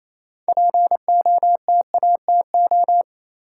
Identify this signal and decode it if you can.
Morse code, heard as a keyed tone, about 28 wpm, 700 Hz. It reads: POTATO
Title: Six Letter Words - 28wpm